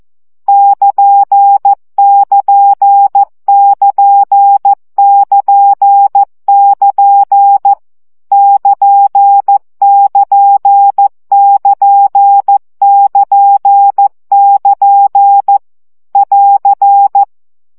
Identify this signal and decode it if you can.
Morse code, heard as a keyed tone, about 14 wpm, 800 Hz. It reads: <KN><KN><KN><KN><KN> <KN><KN><KN><KN><KN> <AR>
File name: CW gyakorlas nyito 12 baud